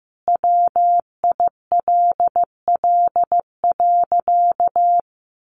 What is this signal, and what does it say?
Morse code, heard as a keyed tone, about 15 wpm, 700 Hz. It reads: WILL.
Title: Sentences from Top 1500 Words 15wpm